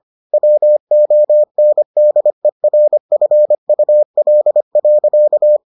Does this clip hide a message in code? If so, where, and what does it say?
Yes: Morse code from 0.3 to 5.6 s, heard as a keyed tone, about 25 wpm, 600 Hz. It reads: WONDERFUL.